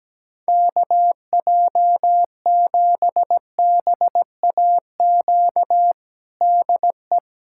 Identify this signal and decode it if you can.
Morse code, heard as a keyed tone, about 17 wpm, 700 Hz. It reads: KJ7BAQ DE
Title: USA Callsign DE Callsign 17wpm